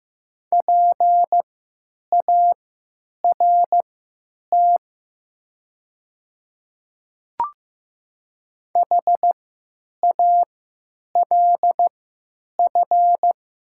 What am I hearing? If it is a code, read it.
Morse code, heard as a keyed tone, about 15 wpm, spaced as for 8 wpm, 700 Hz. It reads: PART HALF
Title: 3x Character Spacing - Top 500 Words - 15wpm